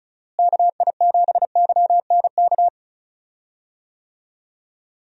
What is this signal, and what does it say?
Morse code, heard as a keyed tone, about 36 wpm, 700 Hz. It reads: KI7YNK
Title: USA Call Signs - 6 Characters - Repeated 3x - 35wpm